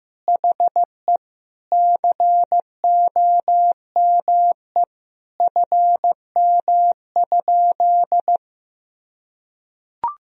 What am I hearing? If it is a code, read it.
Morse code, heard as a keyed tone, about 16 wpm, 700 Hz. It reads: HE COME FM?
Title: Sentences from Top 200 Words and Abbreviations 15wpm